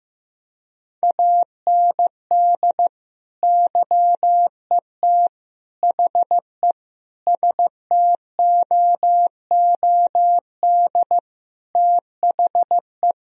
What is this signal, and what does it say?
Morse code, heard as a keyed tone, about 15 wpm, 700 Hz. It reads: AND YET HE STOOD THE
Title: Sentences from Top 1000 Words - Review Part 3 - 15wpm